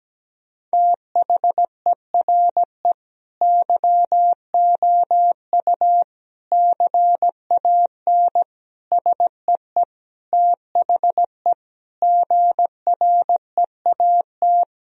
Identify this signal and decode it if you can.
Morse code, heard as a keyed tone, about 17 wpm, 700 Hz. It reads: THERE YOU CAN SEE THE GREAT
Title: Sentences from Top 1000 Words - Review Part 2 - 17wpm